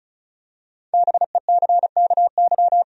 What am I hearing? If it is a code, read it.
Morse code, heard as a keyed tone, about 34 wpm, 700 Hz. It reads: BECKY
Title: USA Common Names - 35wpm